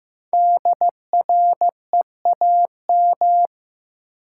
Morse code, heard as a keyed tone, about 15 wpm, 700 Hz. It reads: DREAM
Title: Sentences from Top 1500 Words - New Word Drill - 15wpm